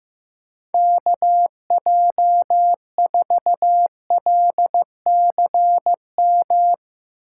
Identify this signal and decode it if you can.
Morse code, heard as a keyed tone, about 15 wpm, 700 Hz. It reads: KJ4LCM
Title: USA Call Signs - 6 Characters - 2x Character Spacing - 15wpm